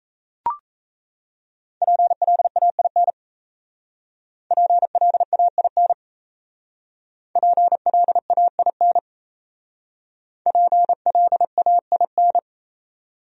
Morse code, heard as a keyed tone, about 34 wpm, 700 Hz. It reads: PLAIN PLAIN PLAIN PLAIN
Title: Speed-Racing Top 500 Words 28wpm